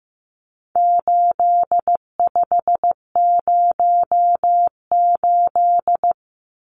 Morse code, heard as a keyed tone, about 15 wpm, 700 Hz. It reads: 8508
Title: Numbers - 4-Digit 15wpm